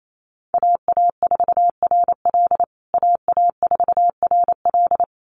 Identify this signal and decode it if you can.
Morse code, heard as a keyed tone, about 28 wpm, 700 Hz. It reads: AA4RL AA4RL
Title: USA Call Signs Repeated 2x 28wpm